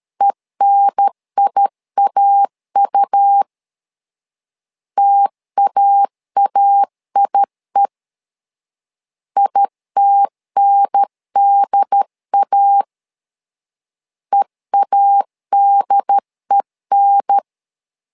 Morse code, heard as a keyed tone, about 13 wpm, 800 Hz. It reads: ENIAU TAAIE ITNDA EADEN
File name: LEZIONI / 2 UHDN / 2 UHDN 12wpm 1456